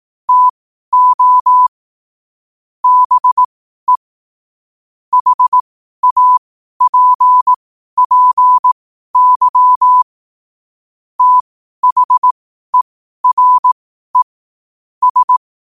Morse code, heard as a keyed tone, about 18 wpm, spaced as for 12 wpm, 1000 Hz. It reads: TO BE HAPPY THERE S
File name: qotdfiles / 12wpm